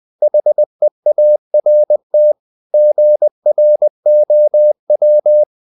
Morse code, heard as a keyed tone, about 20 wpm, 600 Hz. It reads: HEART GROW
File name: CW / Practice / EnglishSayings20